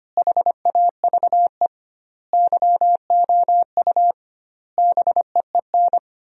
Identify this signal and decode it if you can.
Morse code, heard as a keyed tone, about 25 wpm, 700 Hz. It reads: HAVE YOU BEEN
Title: 2x Word-Spacing - Sentences from Top 200 Words - 25wpm